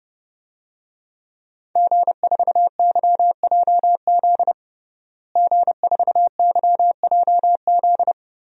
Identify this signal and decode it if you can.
Morse code, heard as a keyed tone, about 30 wpm, 700 Hz. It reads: G4YJZ G4YJZ
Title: DXCC-100 Call Signs - 5 Characters - Repeated 3x - 30wpm